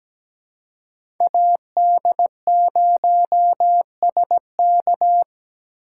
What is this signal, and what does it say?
Morse code, heard as a keyed tone, about 17 wpm, 700 Hz. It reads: AD0SK
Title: USA Call Signs - 5 Characters - 4x Character Spacing - 17wpm